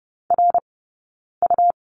Morse code, heard as a keyed tone, about 30 wpm, spaced as for 9 wpm, 690 Hz. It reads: RU
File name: Lesson 19d - Letter w - Two Characters 30wpm